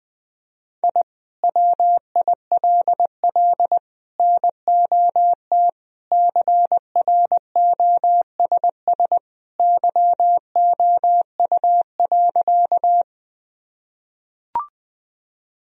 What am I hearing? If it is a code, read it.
Morse code, heard as a keyed tone, about 20 wpm, 700 Hz. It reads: I WILL NOT CROSS YOU.
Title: Sentences from Top 1000 Words - Review Part 1 - 20wpm